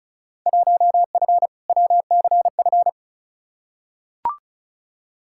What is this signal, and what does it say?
Morse code, heard as a keyed tone, about 34 wpm, 700 Hz. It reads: 1F WCF
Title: Field Day Exchanges - 35wpm